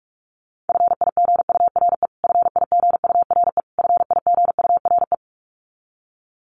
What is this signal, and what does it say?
Morse code, heard as a keyed tone, about 44 wpm, 700 Hz. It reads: FIGURE FIGURE FIGURE
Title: Top 500 Words Repeated 45wpm